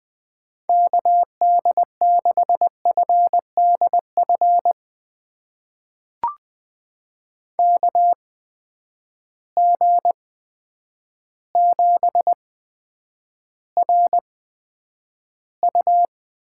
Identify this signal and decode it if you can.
Morse code, heard as a keyed tone, about 20 wpm, spaced as for 9 wpm, 700 Hz. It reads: KD6FDF KG7RU
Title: USA Call Signs - 6 Characters - 8x Character Spacing - 20wpm